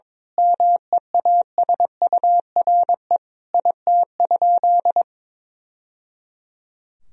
Morse code, heard as a keyed tone, about 22 wpm, 700 Hz. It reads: MEASURE IT?
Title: Sentences from Top 1000 Words - Review Part 2 - 22wpm